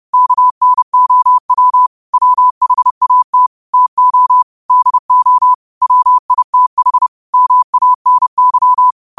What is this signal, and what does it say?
Morse code, heard as a keyed tone, about 30 wpm, 1000 Hz. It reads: KNOW WHAT TO DO WITH MANY